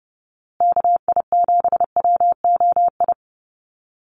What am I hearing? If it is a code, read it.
Morse code, heard as a keyed tone, about 30 wpm, 700 Hz. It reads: KI7WOI
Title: USA Call Signs Repeated 2x 30wpm